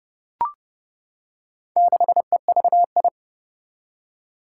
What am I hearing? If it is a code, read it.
Morse code, heard as a keyed tone, about 30 wpm, 700 Hz. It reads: 6EVI
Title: Four Challenging Letter-Number Combinations 30wpm